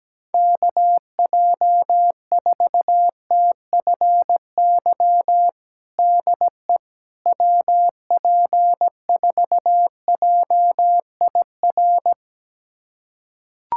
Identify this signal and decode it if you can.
Morse code, heard as a keyed tone, about 17 wpm, 700 Hz. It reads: KJ4TFY DE WP4JIR
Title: USA Callsign DE Callsign 17wpm